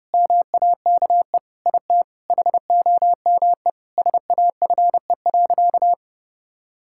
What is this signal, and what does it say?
Morse code, heard as a keyed tone, about 30 wpm, 700 Hz. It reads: MAKE IT HOME SAFE.
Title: Sentences from Top 1600 Words 30wpm